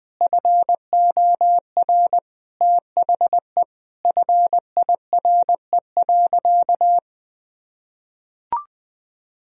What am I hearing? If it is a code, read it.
Morse code, heard as a keyed tone, about 20 wpm, 700 Hz. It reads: FOR THE FIRE.
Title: Sentences from Top 1000 Words - Review Part 2 - 20wpm